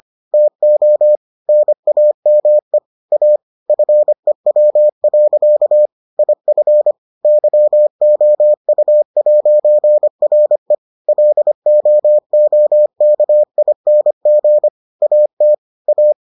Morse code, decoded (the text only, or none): TO NAME A FEW. IF YOU'RE LOOKING AT A